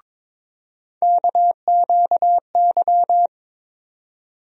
Morse code, heard as a keyed tone, about 22 wpm, 700 Hz. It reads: KQY
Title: USA Call Sign Postfixes - 3 Letters 22wpm